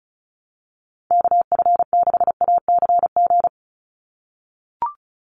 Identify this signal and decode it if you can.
Morse code, heard as a keyed tone, about 34 wpm, 700 Hz. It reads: KF6ACG
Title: USA Call Signs 35wpm